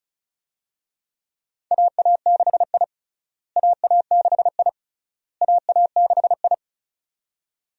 Morse code, heard as a keyed tone, about 34 wpm, 700 Hz. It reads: AA6I AA6I AA6I
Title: USA Call Signs - 4 Characters - Repeated 3x - 35wpm